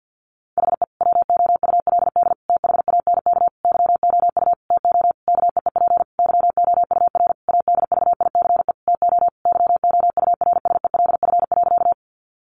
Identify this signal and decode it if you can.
Morse code, heard as a keyed tone, about 50 wpm, 700 Hz. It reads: HE WOULD THANK YOU TO KEEP YOUR ADVICE TO YOURSELF.
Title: Sentences from Top 1100 Words 50wpm